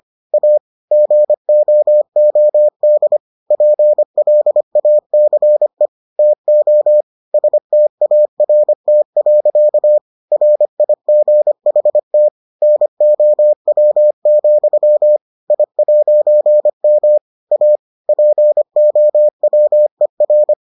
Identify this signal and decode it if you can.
Morse code, heard as a keyed tone, about 25 wpm, 600 Hz. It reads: A GOOD PLACE TO START. RIGHT NOW, I'M A POWER